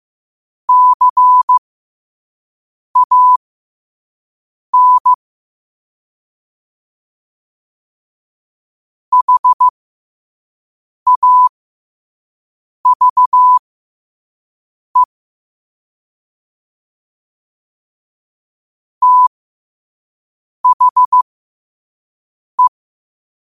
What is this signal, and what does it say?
Morse code, heard as a keyed tone, about 15 wpm, spaced as for 5 wpm, 1000 Hz. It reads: CAN HAVE THE
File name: qotdfiles / 5wpm